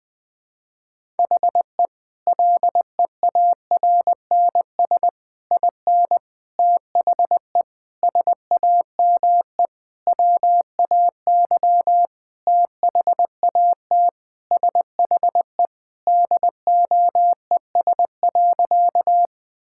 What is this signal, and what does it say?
Morse code, heard as a keyed tone, about 20 wpm, 700 Hz. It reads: HE LEARNS IN THE SAME WAY THAT SHE DOES.
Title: Sentences with 5-Word Phrases - 20wpm